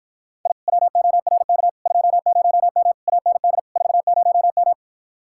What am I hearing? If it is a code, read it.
Morse code, heard as a keyed tone, about 52 wpm, 700 Hz. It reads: I WORK 20M AND 40M